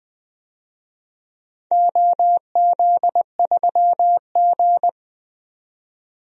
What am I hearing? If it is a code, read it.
Morse code, heard as a keyed tone, about 20 wpm, 700 Hz. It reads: OZ3G
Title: DXCC-100 Call Signs - 4 Characters - 20wpm